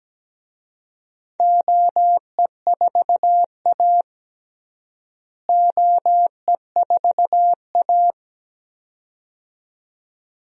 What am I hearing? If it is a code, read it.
Morse code, heard as a keyed tone, about 17 wpm, 700 Hz. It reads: OE4A OE4A
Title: DXCC-100 Call Signs - 4 Characters - Repeated 2x - 17wpm